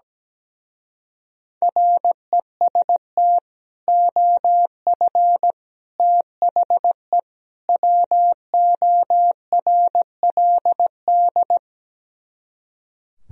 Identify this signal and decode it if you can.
Morse code, heard as a keyed tone, about 17 wpm, 700 Hz. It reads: REST OF THE WORLD
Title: Speed-Racing - Sets of 4 Words 17wpm